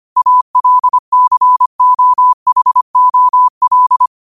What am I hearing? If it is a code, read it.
Morse code, heard as a keyed tone, about 25 wpm, 1000 Hz. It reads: ALCOHOL